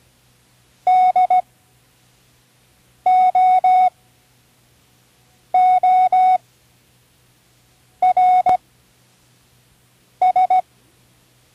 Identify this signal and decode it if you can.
Morse code, heard as a keyed tone, about 17 wpm, spaced as for 5 wpm, 740 Hz. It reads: DOORS
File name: CW Learning Programs / Learning Morse Code (from Tune in the World) / 08 words